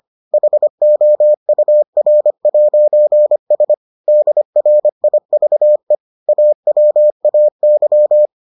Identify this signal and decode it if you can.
Morse code, heard as a keyed tone, about 25 wpm, 600 Hz. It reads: HOUR'S DRIVE AWAY